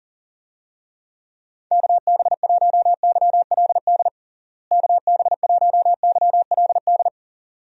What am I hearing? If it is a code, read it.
Morse code, heard as a keyed tone, about 40 wpm, 700 Hz. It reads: KB1YLD KB1YLD
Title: USA Call Signs - 6 Characters - Repeated 2x - 40wpm